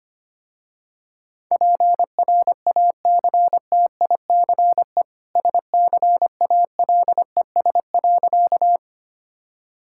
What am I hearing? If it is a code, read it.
Morse code, heard as a keyed tone, about 25 wpm, 700 Hz. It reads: PRACTICE SCALES.
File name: Easy Sentences with Hints - 25wpm